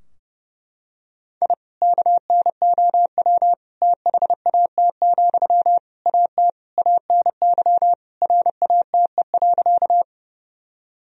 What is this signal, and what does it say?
Morse code, heard as a keyed tone, about 30 wpm, 700 Hz. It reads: I KNOW THAT, AT ANY RATE.
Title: Sentences from Top 1000 Words - Review Part 4 - 30wpm